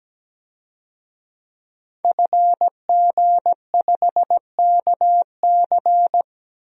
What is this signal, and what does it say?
Morse code, heard as a keyed tone, about 17 wpm, 700 Hz. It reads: FG5KC
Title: DXCC-100 Call Signs - 5 Characters - 17wpm